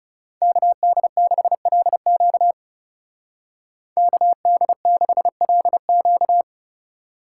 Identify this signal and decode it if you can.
Morse code, heard as a keyed tone, about 34 wpm, 700 Hz. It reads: KD6LQ KD6LQ
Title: Speed Racing - USA 2x2 Call Signs 30wpm